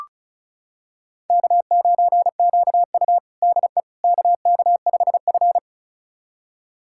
Morse code, heard as a keyed tone, about 36 wpm, 700 Hz. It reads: K9QU DE KK5F
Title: USA Callsign DE Callsign - 4 Characters - 35wpm